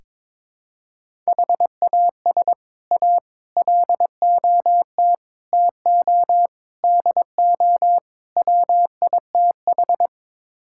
Morse code, heard as a keyed tone, about 22 wpm, 700 Hz. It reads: HAS A LOT TO DO WITH
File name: practice / 22wpm / Sets of 6 Words - 22wpm